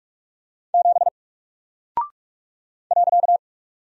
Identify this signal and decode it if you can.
Morse code, heard as a keyed tone, about 44 wpm, 700 Hz. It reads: 7 .
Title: Single Character (Letter:Number:Punctuation:Prosign) Alt 45wpm